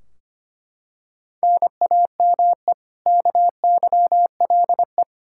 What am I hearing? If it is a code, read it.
Morse code, heard as a keyed tone, about 25 wpm, 700 Hz. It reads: NAME KYLE
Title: Practice QSO - Exchange 1 - Beginning - 25wpm